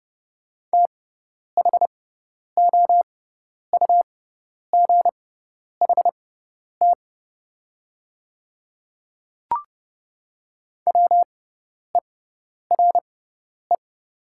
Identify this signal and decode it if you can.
Morse code, heard as a keyed tone, about 30 wpm, spaced as for 10 wpm, 690 Hz. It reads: THOUGHT WERE
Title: 6x Character Spacing - Top 500 Words - 30wpm